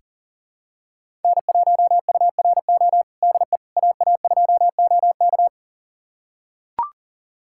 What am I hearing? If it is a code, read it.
Morse code, heard as a keyed tone, about 40 wpm, 700 Hz. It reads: N1URO DE AA2OK
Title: USA Callsign DE Callsign - 5 Characters - 40wpm